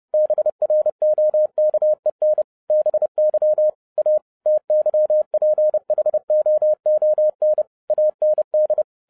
Morse code, heard as a keyed tone, about 30 wpm, 610 Hz. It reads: BROKEN BY A TYPHOON AND